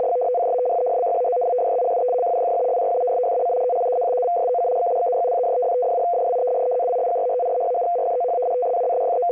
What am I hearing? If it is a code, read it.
Two-tone teleprinter signal, about 200 Hz shift.
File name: F03a- -1800-1805utc-7840khz